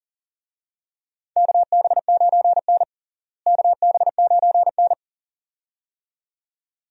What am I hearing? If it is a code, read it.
Morse code, heard as a keyed tone, about 40 wpm, 700 Hz. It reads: KB9N KB9N
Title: USA Call Signs - 4 Characters - Repeated 2x - 40wpm